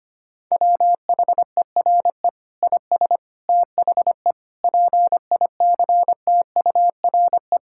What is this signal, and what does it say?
Morse code, heard as a keyed tone, about 25 wpm, 700 Hz. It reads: WHERE IS THE PICTURE
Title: Sentences from Top 1000 Words - Review Part 6 - 25wpm